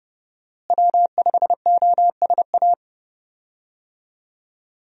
Morse code, heard as a keyed tone, about 30 wpm, 700 Hz. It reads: W5OSA
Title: USA Call Signs - 5 Characters - Repeated 2x - 30wpm